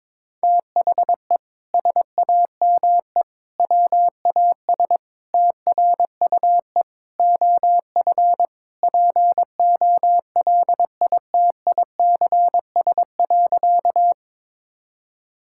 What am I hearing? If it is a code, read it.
Morse code, heard as a keyed tone, about 22 wpm, 700 Hz. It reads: THE SAME WAS TRUE OF POLITICS.
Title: Sentences from Top 1000 Words - Review Part 6 - 22wpm